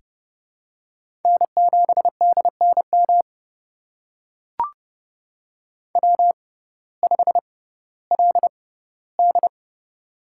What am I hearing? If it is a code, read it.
Morse code, heard as a keyed tone, about 30 wpm, spaced as for 17 wpm, 700 Hz. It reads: N7DNM W5LD
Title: USA Call Signs - 6x Character Spacing - 30wpm